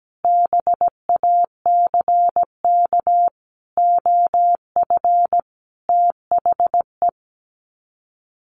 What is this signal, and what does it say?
Morse code, heard as a keyed tone, about 17 wpm, 700 Hz. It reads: BACK OF THE
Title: Sets of 5 Words 17wpm